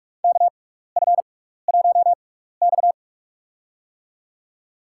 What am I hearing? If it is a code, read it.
Morse code, heard as a keyed tone, about 47 wpm, spaced as for 16 wpm, 700 Hz. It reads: KF1X
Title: USA Call Signs - 4 Characters - 6x Character Spacing - 45wpm